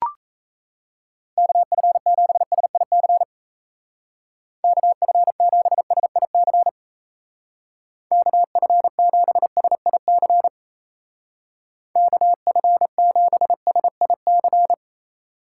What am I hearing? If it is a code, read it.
Morse code, heard as a keyed tone, about 33 wpm, 700 Hz. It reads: KF7SIC KF7SIC KF7SIC KF7SIC
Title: Speed Racing - USA Call Signs 28wpm